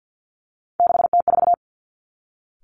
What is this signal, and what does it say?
Morse code, heard as a keyed tone, about 50 wpm, 700 Hz. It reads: 6T4
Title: Three Challenging Letter-Number Combinations 50wpm